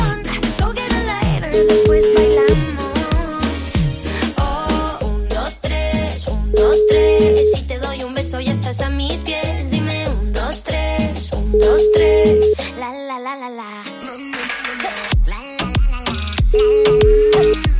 Ringback tone (Europe, 425 Hz), with noise.
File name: AOsoGS19bcy64Ll